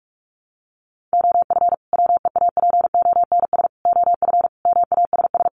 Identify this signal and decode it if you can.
Morse code, heard as a keyed tone, about 44 wpm, 700 Hz. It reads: OF WEAPONS OF MASS
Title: Sets of 5 Words Encore 45wpm